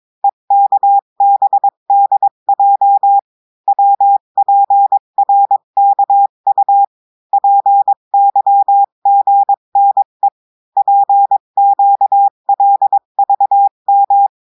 Morse code, heard as a keyed tone, about 22 wpm, 800 Hz. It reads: EKBDJ WPRKU PYGNE PQLVM
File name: cours-du-mercredi / audios / cours-853-du-2-mai-2024-etxte chiffre et texte clair-a-20-wpm